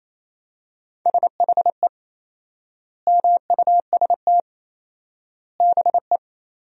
Morse code, heard as a keyed tone, about 28 wpm, 700 Hz. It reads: SHE MUST BE
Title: 4x Word-Spacing - Sentences from Top 400 Words - 28wpm